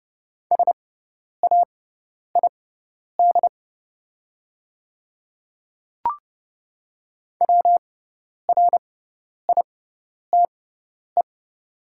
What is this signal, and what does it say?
Morse code, heard as a keyed tone, about 30 wpm, spaced as for 10 wpm, 700 Hz. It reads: SAID WRITE
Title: 6x Character Spacing - Top 100 Words - 30wpm